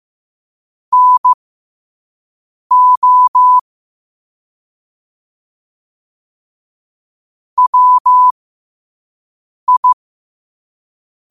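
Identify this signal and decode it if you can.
Morse code, heard as a keyed tone, about 15 wpm, spaced as for 5 wpm, 1000 Hz. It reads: NO WI
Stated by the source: Morse code at 5 wpm.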